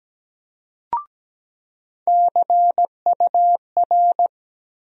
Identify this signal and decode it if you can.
Morse code, heard as a keyed tone, about 17 wpm, 700 Hz. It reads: CUR
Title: USA Call Sign Postfixes - 3 Letters 17wpm